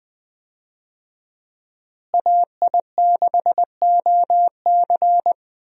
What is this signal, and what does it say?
Morse code, heard as a keyed tone, about 21 wpm, 700 Hz. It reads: AI6OC
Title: USA Call Signs - 5 Characters - Repeated 2x - 20wpm